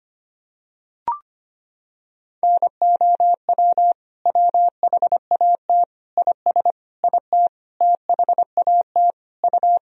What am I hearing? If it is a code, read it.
Morse code, heard as a keyed tone, about 25 wpm, 700 Hz. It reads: NOW WHAT IS IT THAT U
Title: Sentences from Top 200 Words and Abbreviations 25wpm